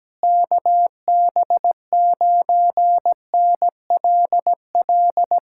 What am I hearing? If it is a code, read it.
Morse code, heard as a keyed tone, about 17 wpm, 700 Hz. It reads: KB9NLL
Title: Speed-Racing Common QSO Phrases - 17wpm